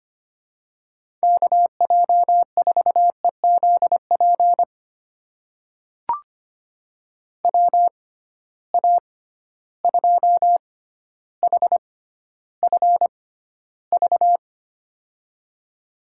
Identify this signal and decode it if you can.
Morse code, heard as a keyed tone, about 25 wpm, spaced as for 13 wpm, 700 Hz. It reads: KJ4EZP WA2HFV
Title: USA Call Signs - 6 Characters - 6x Character Spacing - 25wpm